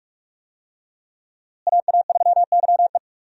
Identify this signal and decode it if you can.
Morse code, heard as a keyed tone, about 46 wpm, 700 Hz. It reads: AA3YE
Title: USA Call Signs - 5 Characters - Repeated 3x - 45wpm